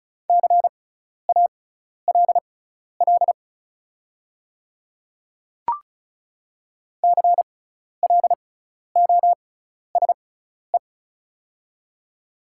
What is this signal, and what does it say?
Morse code, heard as a keyed tone, about 36 wpm, spaced as for 12 wpm, 690 Hz. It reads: CALL CLOSE
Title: 6x Character Spacing - Top 300 Words - 35wpm